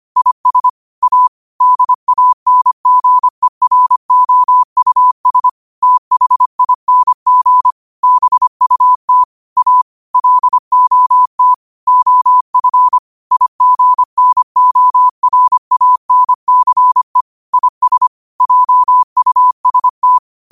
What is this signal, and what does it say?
Morse code, heard as a keyed tone, about 25 wpm, 1000 Hz. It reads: IS A DANGEROUS THING BUT A LOT OF IGNORANCE IS JUST
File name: qotdfiles / 25wpm